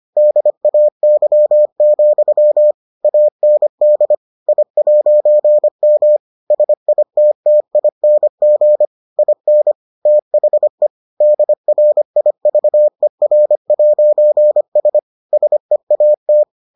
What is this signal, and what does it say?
Morse code, heard as a keyed tone, about 25 wpm, 600 Hz. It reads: DAY, AND I'M SITTING IN THE DRIVER'S SEAT